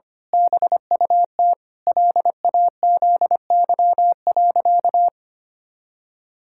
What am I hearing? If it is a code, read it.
Morse code, heard as a keyed tone, about 25 wpm, 700 Hz. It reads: BUT LAZY.
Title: Pangram Challenge v2 - 25wpm